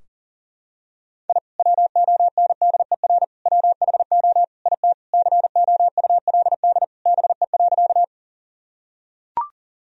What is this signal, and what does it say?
Morse code, heard as a keyed tone, about 40 wpm, 700 Hz. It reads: I WONDER WHO IT COULD BE.
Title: Sentences from Top 1000 Words - Review Part 3 - 40wpm